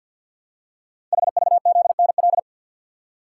Morse code, heard as a keyed tone, about 47 wpm, 700 Hz. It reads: HVZNL
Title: Code Groups - 5 Letters - 50wpm